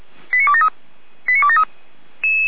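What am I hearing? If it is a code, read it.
Five-tone selective call (ZVEI), heard as quick runs of five tones, tones of about 70 ms. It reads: 89273 89273
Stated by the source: ZVEI five-tone selective call 89522, received with an SDR.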